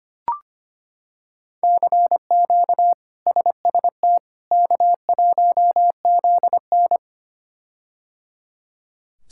Morse code, heard as a keyed tone, about 25 wpm, 700 Hz. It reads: CQ SST K1ZN
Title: K1USN SST QSO Practice - 25wpm